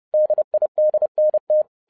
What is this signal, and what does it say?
Morse code, heard as a keyed tone, about 30 wpm, 610 Hz. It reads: DIDNT
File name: 2012-01-mp3 / n11-I DIDNT MISS THE SOLAR CYCLE 24-30wpm